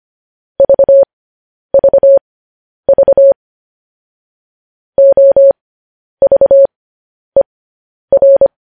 Morse code, heard as a keyed tone, about 25 wpm, spaced as for 10 wpm, 550 Hz. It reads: VVV OVER
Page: news-headlines-in-morse-code-at-10-wpm